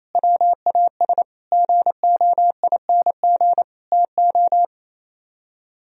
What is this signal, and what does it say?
Morse code, heard as a keyed tone, about 36 wpm, 700 Hz. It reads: WAS GOING TO